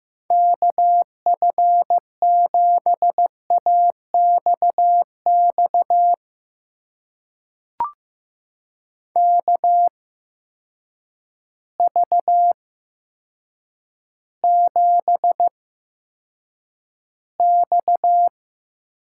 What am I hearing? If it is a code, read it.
Morse code, heard as a keyed tone, about 15 wpm, spaced as for 7 wpm, 700 Hz. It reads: KF7AXX KV7X
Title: USA Call Signs - 8x Character Spacing - 15wpm